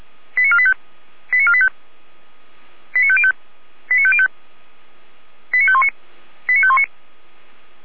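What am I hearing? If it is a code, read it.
Five-tone selective call (ZVEI), heard as quick runs of five tones, tones of about 70 ms. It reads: 89485 89485 89505 89505 89519 89519
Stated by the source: ZVEI five-tone selective call 89416, received by SDR, recorded by Feuerwehr Orlamünde (fire brigade).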